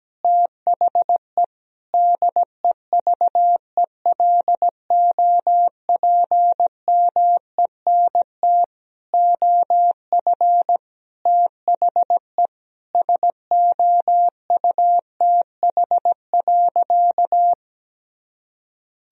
Morse code, heard as a keyed tone, about 17 wpm, 700 Hz. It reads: THE DEVELOPMENT OF THE SOUTH.
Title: Sentences from Top 1000 Words - Review Part 4 - 17wpm